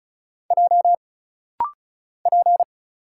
Morse code, heard as a keyed tone, about 34 wpm, 700 Hz. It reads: J P
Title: Single Letter-Number - Rapid-Fire 35wpm